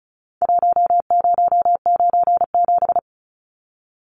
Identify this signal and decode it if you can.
Morse code, heard as a keyed tone, about 34 wpm, 700 Hz. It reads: 1097
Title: Numbers - Contest Style - 4-Digit 35wpm